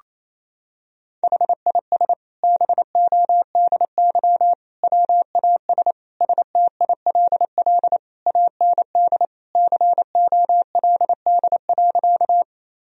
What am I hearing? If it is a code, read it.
Morse code, heard as a keyed tone, about 28 wpm, 700 Hz. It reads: HIS BODY WAS STILL AND COLD.
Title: Sentences from Top 1000 Words - Review Part 2 - 28wpm